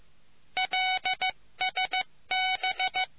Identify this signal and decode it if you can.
Morse code, heard as a keyed tone, about 15 wpm, 750 Hz. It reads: LSB